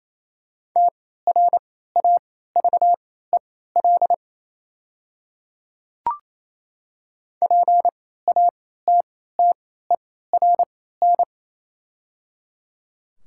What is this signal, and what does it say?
Morse code, heard as a keyed tone, about 28 wpm, spaced as for 16 wpm, 690 Hz. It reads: TRAVEL PATTERN
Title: 3x Character Spacing - Top 500 Words - 28wpm